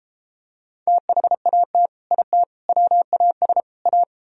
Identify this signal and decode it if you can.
Morse code, heard as a keyed tone, about 32 wpm, 700 Hz. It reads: THAT IT WAS A